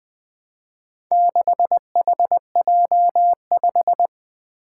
Morse code, heard as a keyed tone, about 20 wpm, 700 Hz. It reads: 6HJ5
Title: Four Challenging Letter-Number Combinations 20wpm